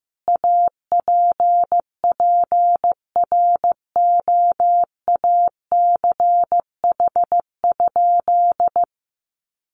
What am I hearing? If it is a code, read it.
Morse code, heard as a keyed tone, about 14 wpm, 700 Hz. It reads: APPROACH?
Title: Sentences from Top 1500 Words 15wpm